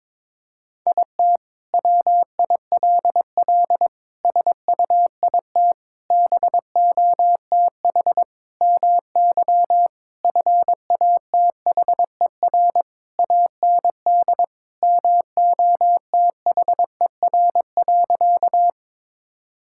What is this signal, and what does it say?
Morse code, heard as a keyed tone, about 22 wpm, 700 Hz. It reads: IT WILL SUIT BOTH MY FATHER AND MOTHER.
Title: Sentences from Top 1600 Words 22wpm